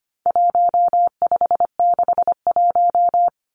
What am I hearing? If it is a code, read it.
Morse code, heard as a keyed tone, about 25 wpm, 700 Hz. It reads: 1561
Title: Numbers - 4-Digit 25wpm